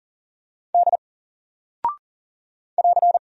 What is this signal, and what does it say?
Morse code, heard as a keyed tone, about 40 wpm, 700 Hz. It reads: D <AR>
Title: Single Character (Letter:Number:Punctuation:Prosign) Alt 40wpm